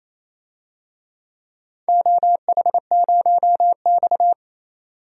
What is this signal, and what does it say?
Morse code, heard as a keyed tone, about 27 wpm, 700 Hz. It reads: OH0X
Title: DXCC-100 Call Signs - 4 Characters - 28wpm